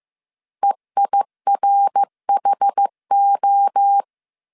Morse code, heard as a keyed tone, about 15 wpm, 800 Hz. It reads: EIRHO